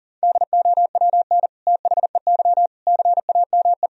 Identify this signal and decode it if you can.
Morse code, heard as a keyed tone, about 40 wpm, 700 Hz. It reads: DOWN THEY CAME
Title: Sentences from Top 1000 Words - Review Part 2 - 40wpm